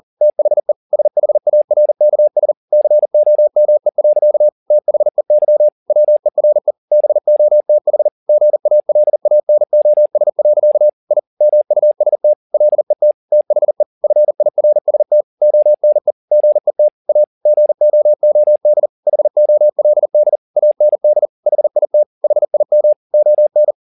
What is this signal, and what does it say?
Morse code, heard as a keyed tone, about 40 wpm, 600 Hz. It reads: THE SHARKS COME. THEY WERE BOTH GALANOS. I MUST LET THE FIRST ONE GET A GOOD HOLD AND HIT HIM ON
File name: the old man and the sea 2 40wpm